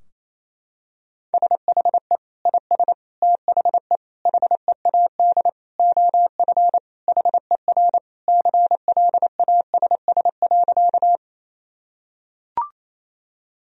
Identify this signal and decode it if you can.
Morse code, heard as a keyed tone, about 28 wpm, 700 Hz. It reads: SHE IS THE HEAD OF HER CLASS.
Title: Sentences from Top 1000 Words - Review Part 2 - 28wpm